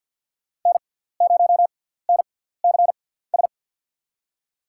Morse code, heard as a keyed tone, about 47 wpm, spaced as for 17 wpm, 690 Hz. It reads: N0NCS
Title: USA Call Signs - 5 Characters - 6x Character Spacing - 50wpm